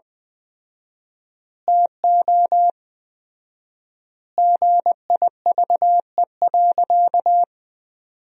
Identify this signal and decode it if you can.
Morse code, heard as a keyed tone, about 20 wpm, 700 Hz. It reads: TO GIVE.
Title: 4x Word-Spacing - Sentences from Top 200 Words - 20wpm